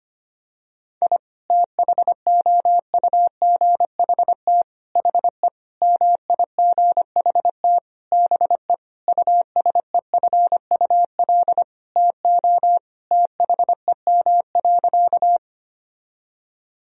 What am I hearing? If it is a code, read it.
Morse code, heard as a keyed tone, about 25 wpm, 700 Hz. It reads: I THOUGHT HE MIGHT BE USEFUL TO THEM.
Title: Sentences from Top 1000 Words - Review Part 4 - 25wpm